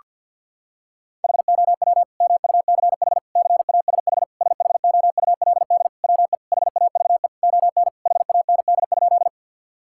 Morse code, heard as a keyed tone, about 51 wpm, 690 Hz. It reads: HOW MUCH CASH SHOULD WE HAVE ON HAND?
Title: Sentences from Top 1000 Words - Review Part 5 - 50wpm